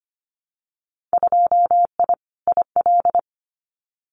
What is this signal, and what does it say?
Morse code, heard as a keyed tone, about 25 wpm, 700 Hz. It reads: 2I IL
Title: Winter Field Day Exchanges 25wpm